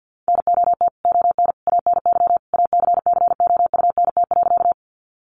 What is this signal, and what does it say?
Morse code, heard as a keyed tone, about 50 wpm, 700 Hz. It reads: NOT ON ANY ACCOUNT.
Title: Sentences from Top 900 Words 50wpm